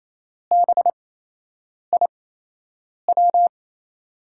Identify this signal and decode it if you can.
Morse code, heard as a keyed tone, about 28 wpm, spaced as for 8 wpm, 700 Hz. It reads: BIW
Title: USA Call Signs - 6 Characters - 8x Character Spacing - 28wpm